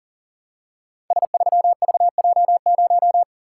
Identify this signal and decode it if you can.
Morse code, heard as a keyed tone, about 38 wpm, 700 Hz. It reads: S3VJ0
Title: Code Groups - 5 Letter:Numbers - 40wpm